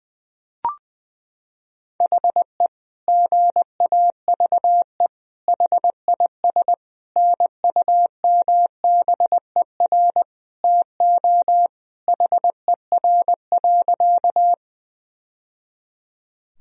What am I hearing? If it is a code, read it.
Morse code, heard as a keyed tone, about 20 wpm, 700 Hz. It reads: HE GAVE HIS NUMBER TO HER.
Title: Sentences from Top 1000 Words - Review Part 1 - 20wpm